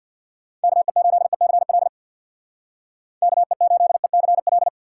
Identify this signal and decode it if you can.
Morse code, heard as a keyed tone, about 64 wpm, 700 Hz. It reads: KE8ECL KE8ECL